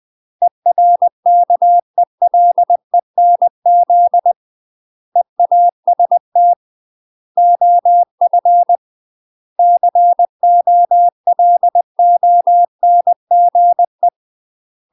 Morse code, heard as a keyed tone, about 20 wpm, 700 Hz. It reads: ERKELENZ EAST OF COLONGE
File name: UK MASTER QSOs TRAINING / 20WPM / UK MASTER QSOs 0058